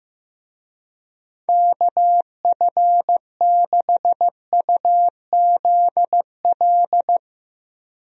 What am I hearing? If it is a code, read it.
Morse code, heard as a keyed tone, about 15 wpm, 700 Hz. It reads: KF6UZL
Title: USA Call Signs - 6 Characters - Repeated 2x - 15wpm